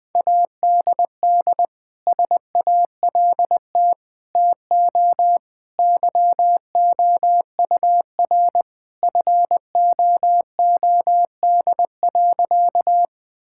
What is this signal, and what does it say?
Morse code, heard as a keyed tone, about 20 wpm, 700 Hz. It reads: ADD SALT TO YOUR FOOD.
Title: Sentences from Top 1000 Words - Review Part 5 - 20wpm